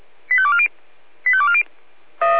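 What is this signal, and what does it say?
Five-tone selective call (ZVEI), heard as quick runs of five tones, tones of about 70 ms. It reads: 85240 85240
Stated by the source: ZVEI five-tone call 85216 received by SDR.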